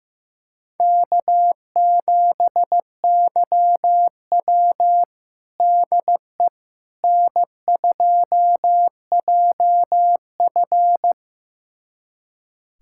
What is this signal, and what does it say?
Morse code, heard as a keyed tone, about 15 wpm, 700 Hz. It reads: K7YW DE N2JF
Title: USA Callsign DE Callsign - 4 Characters - 15wpm